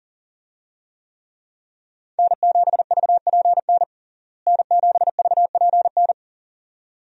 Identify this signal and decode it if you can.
Morse code, heard as a keyed tone, about 40 wpm, 700 Hz. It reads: N7VPN N7VPN
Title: USA Call Signs - 5 Characters - Repeated 2x - 40wpm